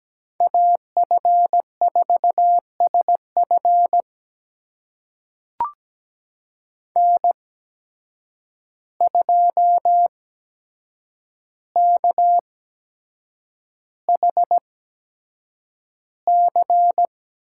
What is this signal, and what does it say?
Morse code, heard as a keyed tone, about 17 wpm, spaced as for 7 wpm, 700 Hz. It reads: AF4SF N2KHC
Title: USA Call Signs - 5 Characters - 8x Character Spacing - 17wpm